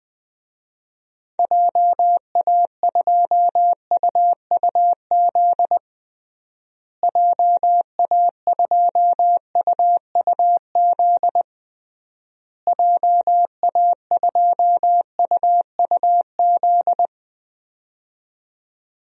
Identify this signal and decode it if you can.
Morse code, heard as a keyed tone, about 20 wpm, 700 Hz. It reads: JA2UUZ JA2UUZ JA2UUZ
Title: DXCC-100 Call Signs - 6 Characters - Repeated 3x - 20wpm